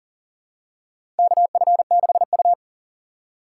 Morse code, heard as a keyed tone, about 40 wpm, 700 Hz. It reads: KF6U
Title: USA Call Signs - 4 Characters - 6x Character Spacing - 40wpm